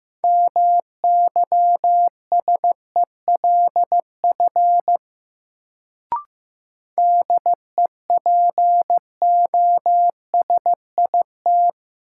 Morse code, heard as a keyed tone, about 15 wpm, 700 Hz. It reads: MYSELF DEPOSIT
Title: Sentences from Top 1300 Words - New Word Drill Alt - 15wpm